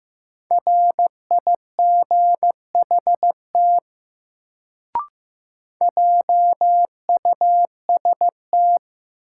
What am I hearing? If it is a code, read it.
Morse code, heard as a keyed tone, about 15 wpm, 700 Hz. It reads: RIGHT JUST
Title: Sentences from Top 200 Words - New Word Drill - 15wpm